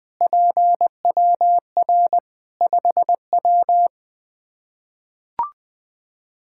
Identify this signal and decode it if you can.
Morse code, heard as a keyed tone, about 20 wpm, 700 Hz. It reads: PWR 5W
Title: Practice QSO - Exchange 2 - Info - 20wpm